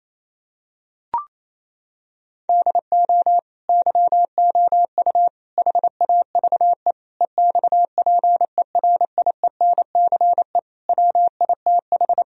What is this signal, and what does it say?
Morse code, heard as a keyed tone, about 27 wpm, 700 Hz. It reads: DO YOU HAVE EXPERIENCE WITH
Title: Sentences from Top 1000 Words - Review Part 3 - 28wpm